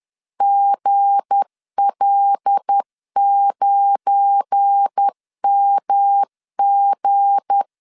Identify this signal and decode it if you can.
Morse code, heard as a keyed tone, about 10 wpm, 800 Hz. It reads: GL9MG
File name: LEZIONI / 7 GF9Z / 7 GF9 10wpm 2747